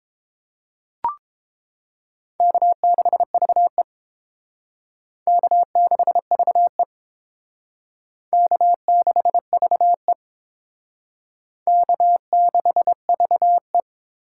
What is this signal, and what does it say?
Morse code, heard as a keyed tone, about 27 wpm, 700 Hz. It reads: K6VE K6VE K6VE K6VE
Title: Speed Racing - USA 1x2 Call Signs 22wpm